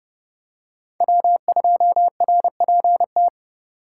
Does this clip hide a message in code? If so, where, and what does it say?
Yes: Morse code from 1.0 to 3.3 s, heard as a keyed tone, about 30 wpm, 700 Hz. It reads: W2RPT